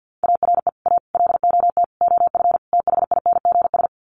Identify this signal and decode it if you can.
Morse code, heard as a keyed tone, about 50 wpm, 700 Hz. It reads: ARE A LOT OF THINGS
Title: Sets of 5 Words Encore2 50wpm